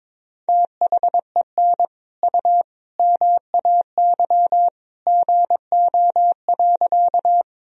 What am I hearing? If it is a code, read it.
Morse code, heard as a keyed tone, about 21 wpm, 700 Hz. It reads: THEN U MAY GO.
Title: Sentences from Top 100 Words and Abbreviations 22wpm